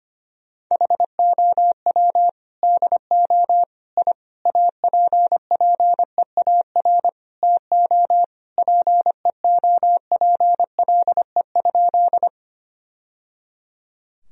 Morse code, heard as a keyed tone, about 25 wpm, 700 Hz. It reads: HOW DO I APPEAR TO PEOPLE?
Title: Sentences from Top 1000 Words - Review Part 2 - 25wpm